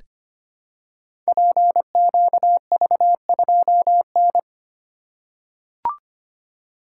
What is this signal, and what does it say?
Morse code, heard as a keyed tone, about 25 wpm, 700 Hz. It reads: PQV2N
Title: Code Groups - 5 Letter:Numbers - 25wpm